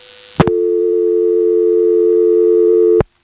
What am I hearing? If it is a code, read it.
Dial tone (North America), 350 + 440 Hz.